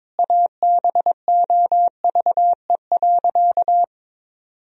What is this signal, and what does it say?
Morse code, heard as a keyed tone, about 22 wpm, 700 Hz. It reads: ABOVE.
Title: Sentences from Top 1000 Words - Review Part 2 - 22wpm